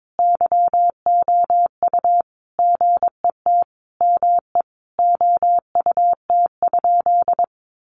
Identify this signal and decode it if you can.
Morse code, heard as a keyed tone, about 22 wpm, 700 Hz. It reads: YOU GET ME OUT?
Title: Sentences from Top 200 Words 22wpm